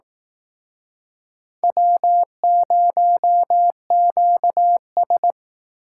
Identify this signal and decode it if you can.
Morse code, heard as a keyed tone, about 19 wpm, 700 Hz. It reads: W0QS
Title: Speed Racing - USA 1x2 Call Signs 15wpm